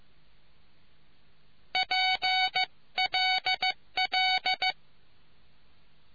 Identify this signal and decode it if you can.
Morse code, heard as a keyed tone, about 15 wpm, 750 Hz. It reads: PLL